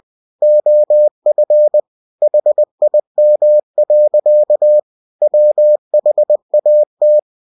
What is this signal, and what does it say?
Morse code, heard as a keyed tone, about 20 wpm, 600 Hz. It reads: OF HIM. WHAT
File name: winnie-20wpm